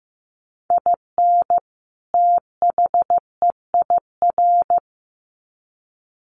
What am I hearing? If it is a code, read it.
Morse code, heard as a keyed tone, about 14 wpm, 700 Hz. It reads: IN THEIR
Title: Sets of 2 Words 15wpm